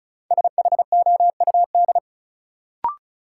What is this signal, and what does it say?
Morse code, heard as a keyed tone, about 36 wpm, 700 Hz. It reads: SHOUD
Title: Sentences from Top 1000 Words - Review Part 4 - New Word Drill - 35wpm